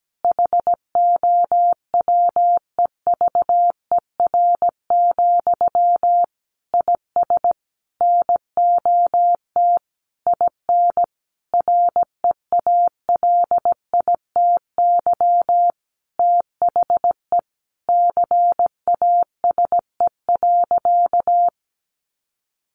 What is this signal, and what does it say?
Morse code, heard as a keyed tone, about 17 wpm, 700 Hz. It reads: HOWEVER, IS NOT IN REALITY THE CASE.
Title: Sentences from Top 1200 Words 17wpm